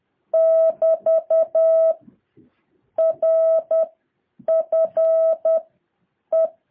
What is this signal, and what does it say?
Morse code, heard as a keyed tone, about 10 wpm, 650 Hz. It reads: <BT> R F E
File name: jh8cbh / 2023cw